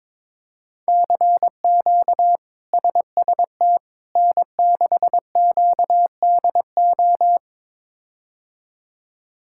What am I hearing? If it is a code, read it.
Morse code, heard as a keyed tone, about 22 wpm, 700 Hz. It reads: CQ SST N6QDO
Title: K1USN SST QSO Practice - 22wpm